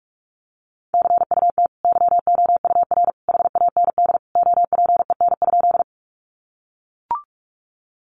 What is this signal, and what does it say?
Morse code, heard as a keyed tone, about 44 wpm, 700 Hz. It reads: CUT YOUR HAND OPEN?
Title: Sentences from Top 300 Words 45wpm